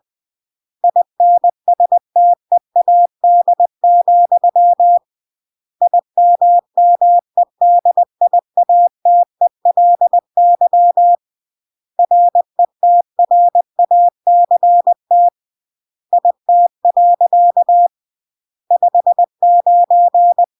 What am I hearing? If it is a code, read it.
Morse code, heard as a keyed tone, about 20 wpm, 700 Hz. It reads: INSTEAD, IMMEDIATELY RETRACT IT. 59
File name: CW / jodorowsky / 82 commandments 20-20